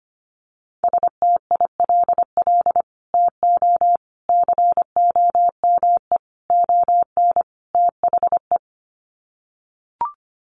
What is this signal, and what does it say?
Morse code, heard as a keyed tone, about 25 wpm, 700 Hz. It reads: STILL TO COME ON THE
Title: Sets of 5 Words Encore2 25wpm